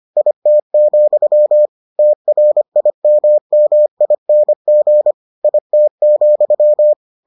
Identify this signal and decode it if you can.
Morse code, heard as a keyed tone, about 25 wpm, 600 Hz. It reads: IT, TRIMMING IT,